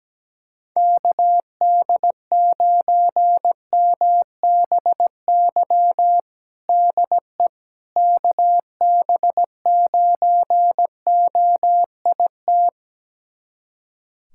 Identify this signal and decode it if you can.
Morse code, heard as a keyed tone, about 17 wpm, 700 Hz. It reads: KD9MBY DE KB9OIT
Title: USA Callsign DE Callsign 17wpm